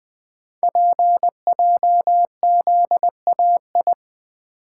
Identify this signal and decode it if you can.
Morse code, heard as a keyed tone, about 20 wpm, 700 Hz. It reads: PJZAI
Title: Code Groups - 5 Letters - 20wpm